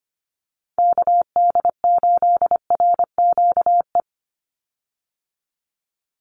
Morse code, heard as a keyed tone, about 25 wpm, 700 Hz. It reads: KD8RQE
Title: Call Signs Repeated 2x - 25wpm